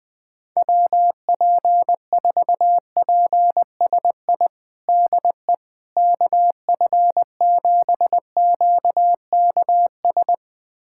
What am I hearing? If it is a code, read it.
Morse code, heard as a keyed tone, about 20 wpm, 700 Hz. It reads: WP4PSI DE KF7QKS
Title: USA Callsign DE Callsign - 6 Characters - 20wpm